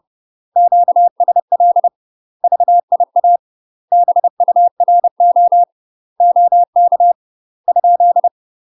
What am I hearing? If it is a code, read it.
Morse code, heard as a keyed tone, about 30 wpm, 700 Hz. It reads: QSL VIA BURO OK ?
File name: UK MASTER QSOs TRAINING / 30WPM / UK MASTER QSOs 0048